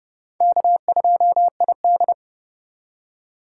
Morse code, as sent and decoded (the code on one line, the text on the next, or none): -.- ..--- .. -..
K2ID